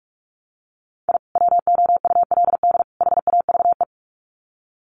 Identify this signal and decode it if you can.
Morse code, heard as a keyed tone, about 44 wpm, 700 Hz. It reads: I WOULD HAVE
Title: Sets of 3 Words 45wpm